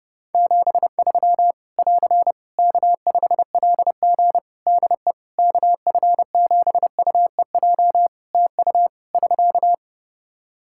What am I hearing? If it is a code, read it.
Morse code, heard as a keyed tone, about 30 wpm, 700 Hz. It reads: 73 <AR> K5LG DE KF7UEJ TU <SK>
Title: K1USN SST QSO Practice 30wpm